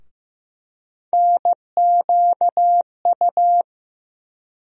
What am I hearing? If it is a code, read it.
Morse code, heard as a keyed tone, about 15 wpm, 700 Hz. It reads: NQU
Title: USA Call Sign Postfixes - 3 Letters 15wpm